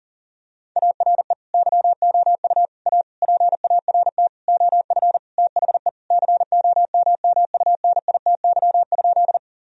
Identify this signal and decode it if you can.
Morse code, heard as a keyed tone, about 37 wpm, 700 Hz. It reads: ARE YOU A PART OF THE COMMUNITY?
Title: Sentences from Top 1000 Words - Review Part 4 - 40wpm